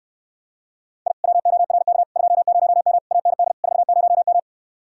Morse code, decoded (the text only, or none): I WORK 20M AND 40M